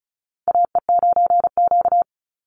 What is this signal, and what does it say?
Morse code, heard as a keyed tone, about 34 wpm, 700 Hz. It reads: AE9Q
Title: NA SOTA Chaser Callsigns 35wpm